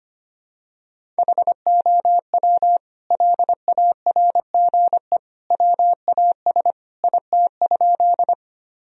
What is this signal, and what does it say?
Morse code, heard as a keyed tone, about 25 wpm, 700 Hz. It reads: HOW LARGE WAS IT?
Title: Sentences from Top 1000 Words - Review Part 1 - 25wpm